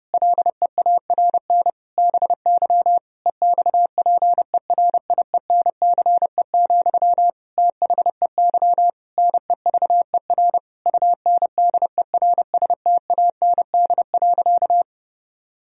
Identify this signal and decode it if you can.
Morse code, heard as a keyed tone, about 32 wpm, 700 Hz. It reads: LEARN BY EXPERIENCE, THEY NEVER UNDERSTAND.
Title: Sentences from Top 1000 Words - Review Part 3 - 30wpm